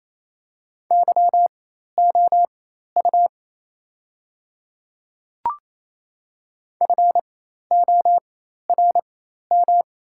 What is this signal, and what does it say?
Morse code, heard as a keyed tone, about 28 wpm, spaced as for 13 wpm, 700 Hz. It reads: YOU FORM
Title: 4x Character Spacing - Top 500 Words - 28wpm